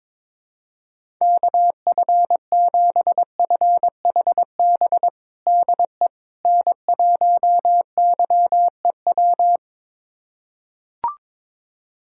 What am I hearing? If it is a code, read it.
Morse code, heard as a keyed tone, about 21 wpm, 700 Hz. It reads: KF7FHB DE N1YEW
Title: USA Callsign DE Callsign 22wpm